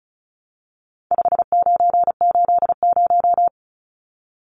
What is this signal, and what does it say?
Morse code, heard as a keyed tone, about 34 wpm, 700 Hz. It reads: 5980
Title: Numbers - 4-Digit 35wpm